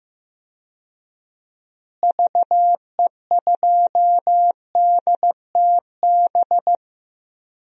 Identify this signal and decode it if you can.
Morse code, heard as a keyed tone, about 16 wpm, 700 Hz. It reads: VE2DTB
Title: DXCC-100 Call Signs - 6 Characters - Repeated 3x - 15wpm